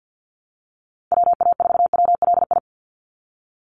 Morse code, heard as a keyed tone, about 50 wpm, 700 Hz. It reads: WA4WLI
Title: Call Signs Repeated 2x - 50wpm